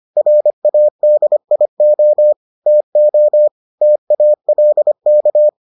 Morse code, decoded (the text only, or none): RADIO TO TALK